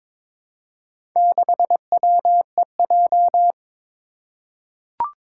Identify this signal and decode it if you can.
Morse code, heard as a keyed tone, about 22 wpm, 700 Hz. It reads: 6WEJ
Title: Four Challenging Letter-Number Combinations 22wpm